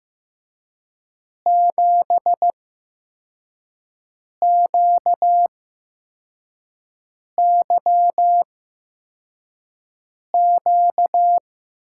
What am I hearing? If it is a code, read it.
Morse code, heard as a keyed tone, about 15 wpm, spaced as for 4 wpm, 700 Hz. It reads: 7QYQ
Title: USA Call Signs - 6 Characters - 8x Character Spacing - 15wpm